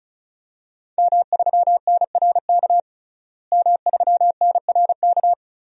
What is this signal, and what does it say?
Morse code, heard as a keyed tone, about 35 wpm, 700 Hz. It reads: M3NRK M3NRK
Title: DXCC-100 Call Signs - 5 Characters - Repeated 3x - 35wpm